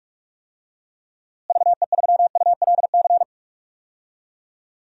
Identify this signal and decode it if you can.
Morse code, heard as a keyed tone, about 46 wpm, 700 Hz. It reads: VE3ULC
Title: DXCC-100 Call Signs - 6 Characters - Repeated 2x - 45wpm